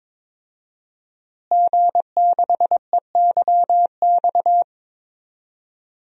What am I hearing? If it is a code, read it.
Morse code, heard as a keyed tone, about 23 wpm, 700 Hz. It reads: G6EYX
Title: DXCC-100 Call Signs - 5 Characters - Repeated 3x - 22wpm